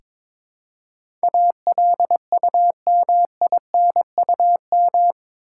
Morse code, heard as a keyed tone, about 21 wpm, 700 Hz. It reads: ALUMINUM
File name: practice / 22wpm / Eight Letter Words - 22wpm